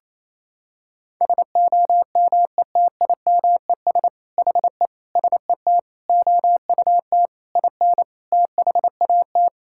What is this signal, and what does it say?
Morse code, heard as a keyed tone, about 28 wpm, 700 Hz. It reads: SOMETIMES HE SET OUT IN THAT
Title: Sentences from Top 1000 Words - Review Part 5 - 28wpm